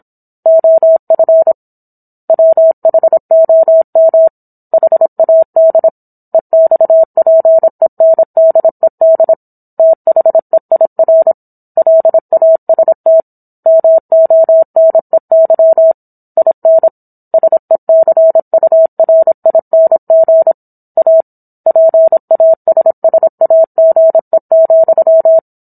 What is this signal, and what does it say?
Morse code, heard as a keyed tone, about 27 wpm, 650 Hz. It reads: OF WHOM HAD EXPENDED THEIR LAST MONEY IN SECURING A PASSAGE,